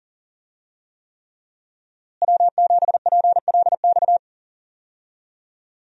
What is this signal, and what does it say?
Morse code, heard as a keyed tone, about 38 wpm, 700 Hz. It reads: W7PLX
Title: USA Call Signs - 5 Characters - Repeated 3x - 40wpm